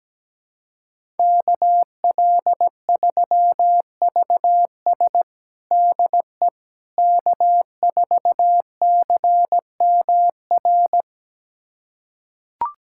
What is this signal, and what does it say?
Morse code, heard as a keyed tone, about 17 wpm, 700 Hz. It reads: KL3VS DE K4CMR
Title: USA Callsign DE Callsign - 5 Characters - 17wpm